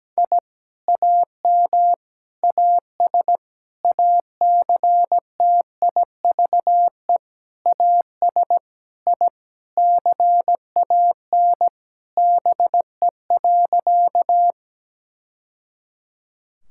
Morse code, heard as a keyed tone, about 17 wpm, 700 Hz. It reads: I AM AS ACTIVE AS I CAN BE.
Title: Sentences from Top 1000 Words - Review Part 6 - 17wpm